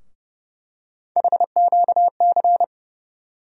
Morse code, heard as a keyed tone, about 30 wpm, 700 Hz. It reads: HQC
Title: USA Call Sign Postfixes - 3 Letters 30wpm